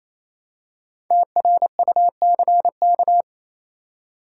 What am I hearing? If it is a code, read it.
Morse code, heard as a keyed tone, about 27 wpm, 700 Hz. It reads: TRUCK
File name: Easy Words - Speed-Racing with Hints - 22wpm